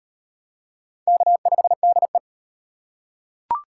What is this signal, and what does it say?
Morse code, heard as a keyed tone, about 34 wpm, 700 Hz. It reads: K5DE